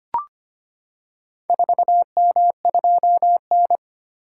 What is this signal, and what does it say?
Morse code, heard as a keyed tone, about 25 wpm, 700 Hz. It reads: 4M2N
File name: practice / 25wpm / Four Letter-Number Combinations - 25wpm